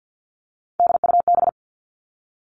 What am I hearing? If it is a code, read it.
Morse code, heard as a keyed tone, about 50 wpm, 700 Hz. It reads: DUB
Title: Three Challenging Letter-Number Combinations 50wpm